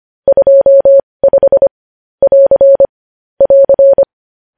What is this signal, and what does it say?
Morse code, heard as a keyed tone, about 25 wpm, spaced as for 20 wpm, 550 Hz. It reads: 25 <AR> <AR>
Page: news-headlines-in-morse-code-at-20-wpm